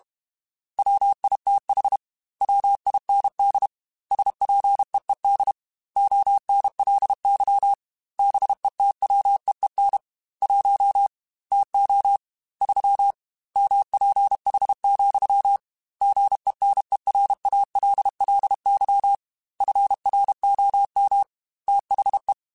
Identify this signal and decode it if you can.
Morse code, heard as a keyed tone, about 32 wpm, 800 Hz. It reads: WITH WIND SPEED ONLY BETWEEN 1 TO 3 MPH, GENERALLY FROM THE